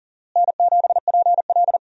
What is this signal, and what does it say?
Morse code, heard as a keyed tone, about 40 wpm, 700 Hz. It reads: N7PL
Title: USA Call Signs - 4 Characters - Repeated 3x - 40wpm